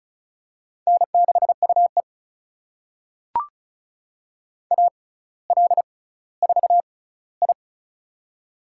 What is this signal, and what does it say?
Morse code, heard as a keyed tone, about 34 wpm, spaced as for 18 wpm, 700 Hz. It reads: N6UE AL4I
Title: USA Call Signs - 4 Characters - 6x Character Spacing - 35wpm